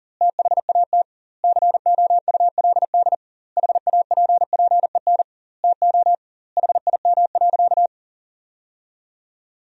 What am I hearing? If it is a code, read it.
Morse code, heard as a keyed tone, about 37 wpm, 700 Hz. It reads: THAT COULD HAPPEN TO HIM.
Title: 2x Word-Spacing - Sentences from Top 400 Words - 40wpm